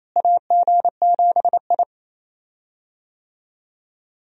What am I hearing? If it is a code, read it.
Morse code, heard as a keyed tone, about 28 wpm, 700 Hz. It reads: AG7I
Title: USA Call Signs - 4 Characters - Repeated 3x - 28wpm